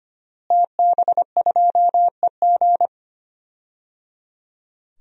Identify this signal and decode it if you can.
Morse code, heard as a keyed tone, about 25 wpm, 700 Hz. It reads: TB2EG
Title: DXCC-100 Call Signs - 5 Characters - Repeated 3x - 25wpm